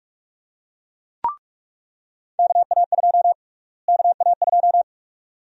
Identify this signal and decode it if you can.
Morse code, heard as a keyed tone, about 46 wpm, 700 Hz. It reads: KA2 KA2
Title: USA Call Sign Prefixes - Repeated 2x - 45wpm